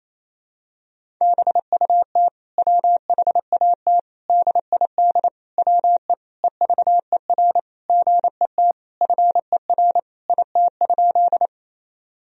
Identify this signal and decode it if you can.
Morse code, heard as a keyed tone, about 28 wpm, 700 Hz. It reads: BUT WHAT DID WE EVER GET FER IT?
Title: Sentences from Top 400 Words and Abbreviations 28wpm